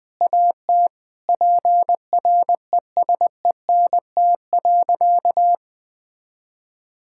Morse code, heard as a keyed tone, about 20 wpm, 700 Hz. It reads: AT PRESENT.
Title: Sentences from Top 1600 Words 20wpm